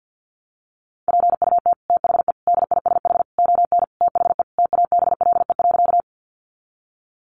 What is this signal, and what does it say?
Morse code, heard as a keyed tone, about 50 wpm, 700 Hz. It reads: PUT THE DISH ON THE TABLE.
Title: Sentences from Top 1300 Words 50wpm